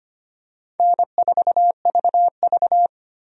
Morse code, heard as a keyed tone, about 25 wpm, 700 Hz. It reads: N4VV
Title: USA Call Signs - 4 Characters - 25wpm